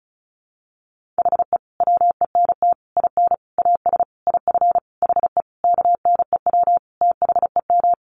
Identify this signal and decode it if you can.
Morse code, heard as a keyed tone, about 34 wpm, 700 Hz. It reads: HE WENT IN AS IF HE KNEW THEM
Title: Sentences from Top 1400 Words 35wpm